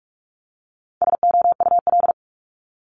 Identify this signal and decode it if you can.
Morse code, heard as a keyed tone, about 46 wpm, 700 Hz. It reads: SOUL
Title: Multi-letter Phonemes Repeated 3x 45wpm